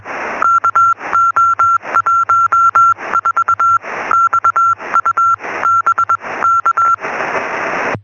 Morse code, heard as a keyed tone, about 21 wpm, spaced as for 18 wpm, 1360 Hz. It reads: KO14XU BK
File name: sm7lcb / 1957 ly2aa 23cm